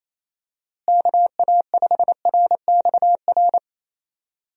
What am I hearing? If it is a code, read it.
Morse code, heard as a keyed tone, about 28 wpm, 700 Hz. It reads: KA5RXR
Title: USA Call Signs - 6 Characters - Repeated 3x - 28wpm